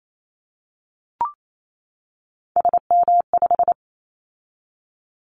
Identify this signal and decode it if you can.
Morse code, heard as a keyed tone, about 28 wpm, 700 Hz. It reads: SM5
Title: Callsign Prefixes for Top 25 DXCC - 28wpm